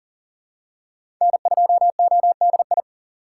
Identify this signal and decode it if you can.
Morse code, heard as a keyed tone, about 39 wpm, 700 Hz. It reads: N2ODI
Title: USA Call Signs - 5 Characters - 4x Character Spacing - 40wpm